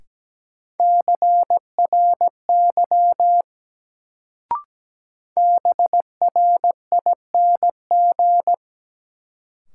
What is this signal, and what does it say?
Morse code, heard as a keyed tone, about 17 wpm, 700 Hz. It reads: CRY BRING
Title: Sets of 5 Words Encore - New Word Drill - 17wpm